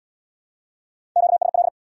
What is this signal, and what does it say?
Morse code, heard as a keyed tone, about 62 wpm, 700 Hz. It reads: QSL
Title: Speed-Racing Q-Codes - 50wpm